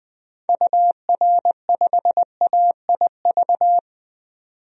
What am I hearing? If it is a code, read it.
Morse code, heard as a keyed tone, about 20 wpm, 700 Hz. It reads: UR5AIV
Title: DXCC-100 Call Signs - 6 Characters - 20wpm